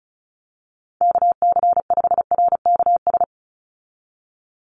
Morse code, heard as a keyed tone, about 34 wpm, 700 Hz. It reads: KC5RKS
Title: USA Call Signs 35wpm